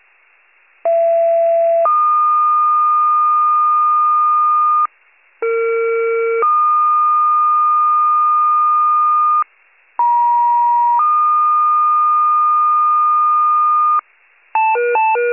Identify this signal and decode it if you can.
Two-tone page, heard as a one-second tone, then a three-second tone, about 670 then 1220 Hz, about 460 then 1220 Hz, about 950 then 1220 Hz. It is heard from 0.8 to 14.0 s.
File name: pager / 83 RESCUE / wcalert